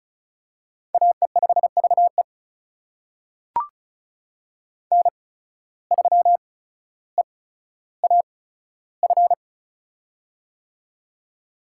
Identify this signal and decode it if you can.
Morse code, heard as a keyed tone, about 34 wpm, spaced as for 15 wpm, 700 Hz. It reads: AE5VE N3EAF
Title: USA Call Signs - 5 Characters - 8x Character Spacing - 35wpm